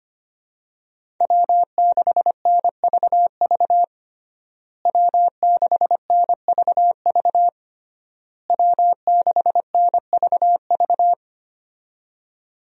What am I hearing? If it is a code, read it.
Morse code, heard as a keyed tone, about 25 wpm, 700 Hz. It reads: W6NVV W6NVV W6NVV
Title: USA Call Signs - 5 Characters - Repeated 3x - 25wpm